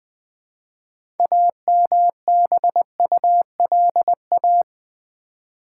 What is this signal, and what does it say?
Morse code, heard as a keyed tone, about 20 wpm, 700 Hz. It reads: AMBULA
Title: Build A Word - 3 Letters At A Time - 9-Letter Words - 20wpm